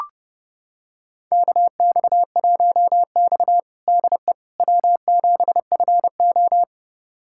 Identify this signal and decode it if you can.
Morse code, heard as a keyed tone, about 30 wpm, 700 Hz. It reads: KX1X DE W7FO
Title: USA Callsign DE Callsign - 4 Characters - 30wpm